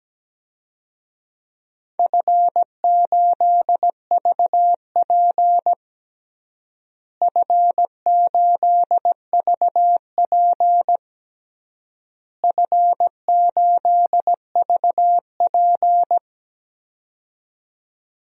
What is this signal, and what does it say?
Morse code, heard as a keyed tone, about 17 wpm, 700 Hz. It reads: F8VP F8VP F8VP
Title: DXCC-100 Call Signs - 4 Characters - Repeated 3x - 17wpm